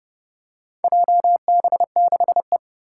Morse code, heard as a keyed tone, about 28 wpm, 700 Hz. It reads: JB6E
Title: Four Challenging Letter-Number Combinations 30wpm